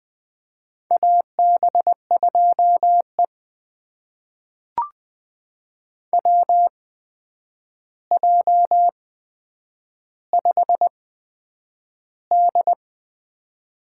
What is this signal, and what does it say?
Morse code, heard as a keyed tone, about 20 wpm, spaced as for 9 wpm, 700 Hz. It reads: AB2E WJ5D
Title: USA Call Signs - 4 Characters - 8x Character Spacing - 20wpm